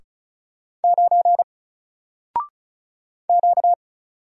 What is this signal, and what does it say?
Morse code, heard as a keyed tone, about 36 wpm, 700 Hz. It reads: 9 Q
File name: Single-Letter-Number-35wpm